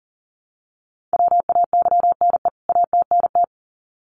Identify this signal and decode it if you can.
Morse code, heard as a keyed tone, about 40 wpm, 700 Hz. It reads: WAYNE ATNT
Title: CWT Exchange 40wpm